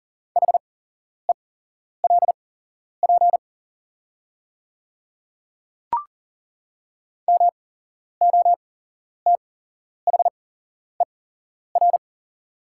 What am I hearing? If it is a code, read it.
Morse code, heard as a keyed tone, about 40 wpm, spaced as for 11 wpm, 700 Hz. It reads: HELP MOTHER
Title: 8x Character Spacing - Top 300 Words - 40wpm